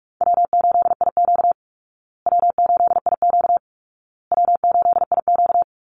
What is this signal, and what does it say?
Morse code, heard as a keyed tone, about 44 wpm, 700 Hz. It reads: W8IQ W8IQ W8IQ
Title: Easy Call Signs Repeated 3x - 45wpm